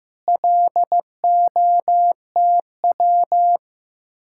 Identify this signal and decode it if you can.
Morse code, heard as a keyed tone, about 15 wpm, 700 Hz. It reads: LOTW
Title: Speed-Racing Common QSO Phrases Encore - 15wpm